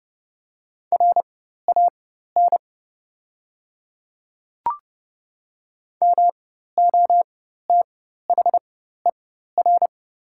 Morse code, heard as a keyed tone, about 30 wpm, spaced as for 14 wpm, 700 Hz. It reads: RAN MOTHER
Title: 4x Character Spacing - Top 500 Words - 30wpm